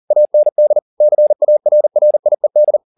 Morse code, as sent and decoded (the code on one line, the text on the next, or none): .- -. -.. / -.-. .- .-. .-. .. -..
AND CARRIED